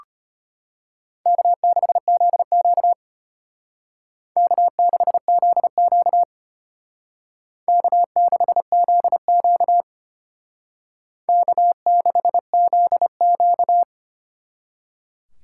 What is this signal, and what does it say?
Morse code, heard as a keyed tone, about 33 wpm, 700 Hz. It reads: K6ZQ K6ZQ K6ZQ K6ZQ